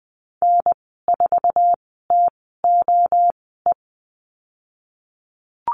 Morse code, heard as a keyed tone, about 20 wpm, spaced as for 15 wpm, 700 Hz. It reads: N4TOE
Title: 055 Lesson 10g - Letter r - Call Signs 20wpm